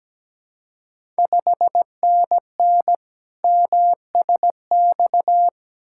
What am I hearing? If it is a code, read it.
Morse code, heard as a keyed tone, about 17 wpm, 700 Hz. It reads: 5NN MSX
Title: Ontario QSO Party Exchange 17wpm